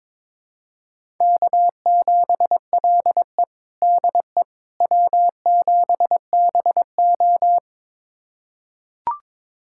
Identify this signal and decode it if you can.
Morse code, heard as a keyed tone, about 22 wpm, 700 Hz. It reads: K7LE DE W7BO
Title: USA Callsign DE Callsign - 4 Characters - 22wpm